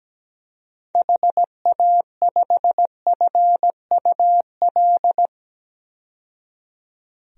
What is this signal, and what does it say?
Morse code, heard as a keyed tone, about 17 wpm, 700 Hz. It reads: HA5FUL
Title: DXCC-100 Call Signs - 6 Characters - 17wpm